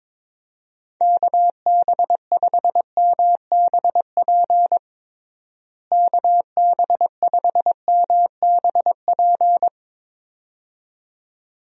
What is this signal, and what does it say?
Morse code, heard as a keyed tone, about 22 wpm, 700 Hz. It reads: KB5MBP KB5MBP
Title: USA Call Signs - 6 Characters - Repeated 2x - 22wpm